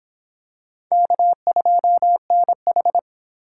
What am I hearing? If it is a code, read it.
Morse code, heard as a keyed tone, about 25 wpm, 700 Hz. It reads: K2NH